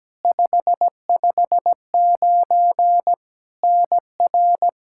Morse code, heard as a keyed tone, about 17 wpm, 700 Hz. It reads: 559 NR
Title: K1USN SST QSO Practice 17wpm